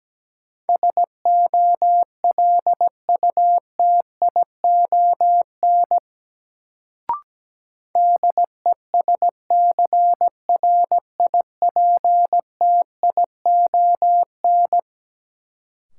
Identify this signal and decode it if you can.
Morse code, heard as a keyed tone, about 17 wpm, 700 Hz. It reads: SOLUTION DESCRIPTION
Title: Sentences from Top 1300 Words - New Word Drill Alt - 17wpm